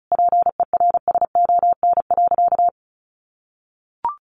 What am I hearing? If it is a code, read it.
Morse code, heard as a keyed tone, about 34 wpm, 700 Hz. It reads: PERSON.
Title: Sentences from Top 1500 Words 35wpm